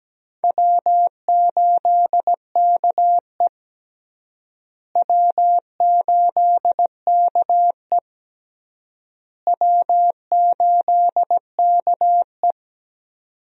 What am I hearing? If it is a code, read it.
Morse code, heard as a keyed tone, about 17 wpm, 700 Hz. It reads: W8KE W8KE W8KE
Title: USA Call Signs - 4 Characters - Repeated 3x - 17wpm